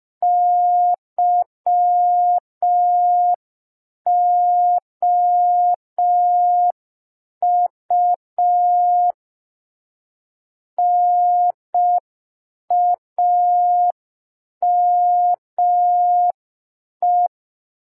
Morse code, heard as a keyed tone, about 5 wpm, 710 Hz. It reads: YOU NAME